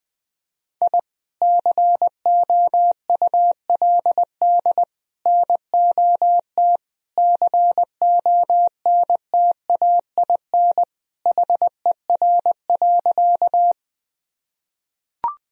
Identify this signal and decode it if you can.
Morse code, heard as a keyed tone, about 21 wpm, 700 Hz. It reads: I COULD NOT CONTAIN HER.
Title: Sentences from Top 1000 Words - Review Part 2 - 20wpm